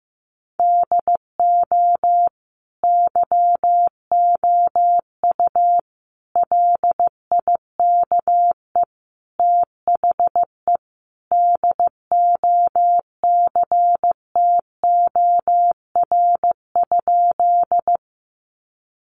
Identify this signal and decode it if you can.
Morse code, heard as a keyed tone, about 15 wpm, 700 Hz. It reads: DO YOU LIKE THE DOCTOR?
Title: Sentences from Top 1500 Words 15wpm